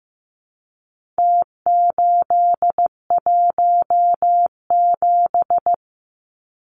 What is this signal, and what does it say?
Morse code, heard as a keyed tone, about 15 wpm, 700 Hz. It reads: T817
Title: Numbers - Contest Style - 4-Digit 15wpm